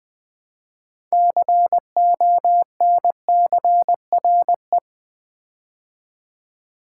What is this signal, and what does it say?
Morse code, heard as a keyed tone, about 20 wpm, 700 Hz. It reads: CONCRE
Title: Build A Word - 2 Letters At A Time - 8-Letter Words - 20wpm